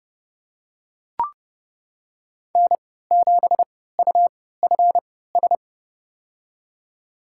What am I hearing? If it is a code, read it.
Morse code, heard as a keyed tone, about 30 wpm, spaced as for 17 wpm, 700 Hz. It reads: N7UFS
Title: USA Call Signs - 5 Characters - 3x Character Spacing - 30wpm